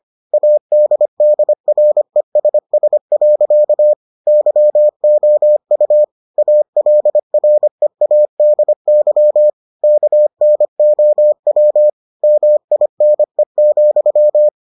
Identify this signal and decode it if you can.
Morse code, heard as a keyed tone, about 25 wpm, 600 Hz. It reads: ADDRESS. YOU ALREADY KNOW MINE,